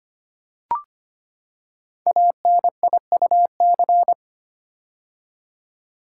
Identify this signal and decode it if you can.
Morse code, heard as a keyed tone, about 25 wpm, 700 Hz. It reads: ANIUC
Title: Code Groups - 5 Letter:Numbers - 25wpm